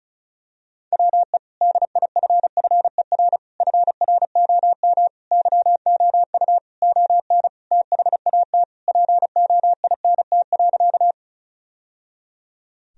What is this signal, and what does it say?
Morse code, heard as a keyed tone, about 34 wpm, 700 Hz. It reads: WE DIFFER FROM YOU ON THAT POINT.
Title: Sentences from Top 1000 Words - Review Part 1 - 35wpm